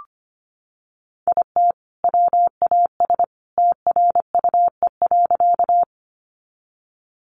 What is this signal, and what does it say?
Morse code, heard as a keyed tone, about 25 wpm, 700 Hz. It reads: IT WAS TRUE.
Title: Sentences from Top 400 Words 25wpm